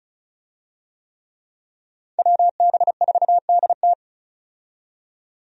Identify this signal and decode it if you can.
Morse code, heard as a keyed tone, about 36 wpm, 700 Hz. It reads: WB4DT
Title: USA Call Signs - 5 Characters - 35wpm